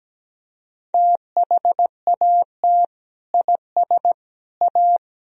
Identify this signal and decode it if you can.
Morse code, heard as a keyed tone, about 17 wpm, 700 Hz. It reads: THAT IS A
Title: Sentences from Top 1000 Words - Review Part 3 - 17wpm